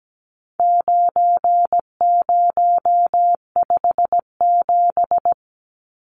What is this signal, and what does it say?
Morse code, heard as a keyed tone, about 17 wpm, 700 Hz. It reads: 9057
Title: Numbers - 4-Digit 17wpm